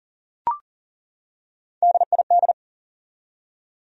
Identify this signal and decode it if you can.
Morse code, heard as a keyed tone, about 40 wpm, 700 Hz. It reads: DID
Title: Top 100 Words 40wpm